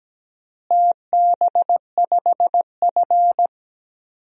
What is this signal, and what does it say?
Morse code, heard as a keyed tone, about 17 wpm, 700 Hz. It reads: TB5F
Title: DXCC-100 Call Signs - 4 Characters - Repeated 3x - 17wpm